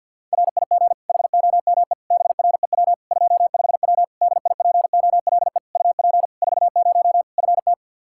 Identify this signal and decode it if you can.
Morse code, heard as a keyed tone, about 52 wpm, 700 Hz. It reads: RIG HOME BREW 25W DIPOLE UP 40 FT
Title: K1USN SST QSO Practice 50wpm